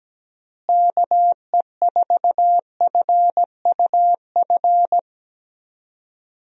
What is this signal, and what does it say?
Morse code, heard as a keyed tone, about 17 wpm, 700 Hz. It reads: KE4FUF
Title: USA Call Signs - 6 Characters - Repeated 2x - 17wpm